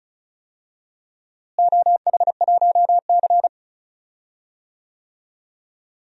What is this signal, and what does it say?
Morse code, heard as a keyed tone, about 34 wpm, 700 Hz. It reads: OH1C
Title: DXCC-100 Call Signs - 4 Characters - Repeated 3x - 35wpm